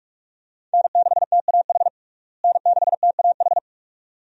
Morse code, heard as a keyed tone, about 46 wpm, 700 Hz. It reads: N6TAH N6TAH
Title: USA Call Signs - 5 Characters - Repeated 2x - 45wpm